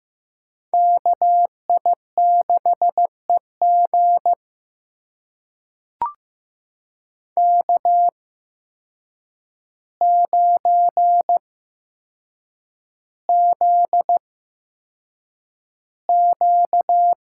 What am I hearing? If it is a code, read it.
Morse code, heard as a keyed tone, about 15 wpm, spaced as for 7 wpm, 700 Hz. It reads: KI6EG K9ZQ
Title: USA Call Signs - 8x Character Spacing - 15wpm